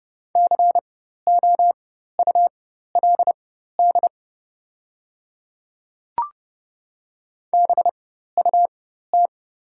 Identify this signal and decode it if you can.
Morse code, heard as a keyed tone, about 30 wpm, spaced as for 14 wpm, 690 Hz. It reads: COULD BUT
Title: 4x Character Spacing - Top 100 Words - 30wpm